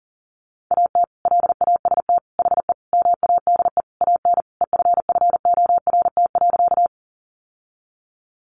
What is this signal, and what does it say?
Morse code, heard as a keyed tone, about 40 wpm, 700 Hz. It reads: AT LAST HE MADE AN EFFORT.
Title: Sentences from Top 1200 Words 40wpm